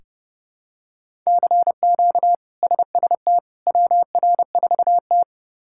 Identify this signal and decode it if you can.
Morse code, heard as a keyed tone, about 30 wpm, 700 Hz. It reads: CQ SST WR4T
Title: K1USN SST QSO Practice - 30wpm